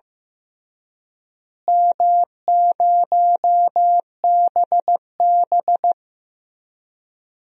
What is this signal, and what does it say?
Morse code, heard as a keyed tone, about 15 wpm, 700 Hz. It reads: M0BB
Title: DXCC-100 Call Signs - 4 Characters - Repeated 3x - 15wpm